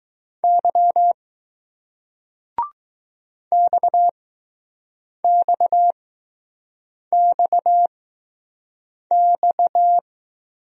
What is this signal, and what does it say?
Morse code, heard as a keyed tone, about 20 wpm, spaced as for 6 wpm, 700 Hz. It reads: Y XXXX